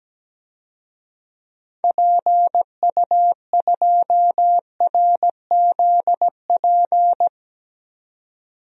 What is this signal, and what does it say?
Morse code, heard as a keyed tone, about 17 wpm, 700 Hz. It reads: PU2RZP
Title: DXCC-100 Call Signs - 6 Characters - 17wpm